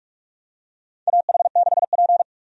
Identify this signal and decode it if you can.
Morse code, heard as a keyed tone, about 43 wpm, 700 Hz. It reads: AH6P
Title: USA Call Signs - 4 Characters - 45wpm